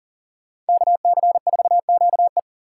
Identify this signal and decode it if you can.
Morse code, heard as a keyed tone, about 40 wpm, 700 Hz. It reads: KC4QE
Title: USA Call Signs - 5 Characters - 3x Character Spacing - 40wpm